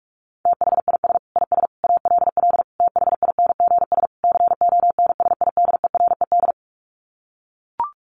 Morse code, heard as a keyed tone, about 44 wpm, 700 Hz. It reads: THIS IS ALL THINGS CONSIDERED
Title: Sets of 5 Words 45wpm